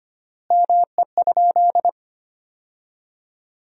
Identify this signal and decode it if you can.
Morse code, heard as a keyed tone, about 25 wpm, 700 Hz. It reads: ME?
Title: 3x Word-Spacing - Sentences from Top 200 Words - 25wpm